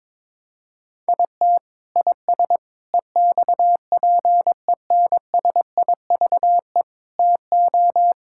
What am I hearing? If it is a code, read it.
Morse code, heard as a keyed tone, about 23 wpm, 700 Hz. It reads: IT IS EXPENSIVE TO
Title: Sentences from Top 1000 Words - Review Part 6 - 22wpm